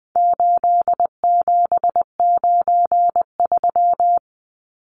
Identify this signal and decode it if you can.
Morse code, heard as a keyed tone, about 20 wpm, 700 Hz. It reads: 8793
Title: Numbers - 4-Digit 20wpm